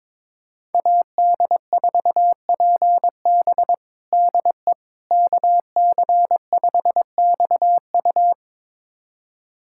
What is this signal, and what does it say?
Morse code, heard as a keyed tone, about 22 wpm, 700 Hz. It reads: AD4PB DE KC5XU
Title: USA Callsign DE Callsign - 5 Characters - 22wpm